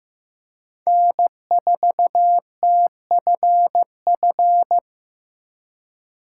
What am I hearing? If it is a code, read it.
Morse code, heard as a keyed tone, about 16 wpm, 700 Hz. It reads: N4TFF
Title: USA Call Signs - 5 Characters - 3x Character Spacing - 15wpm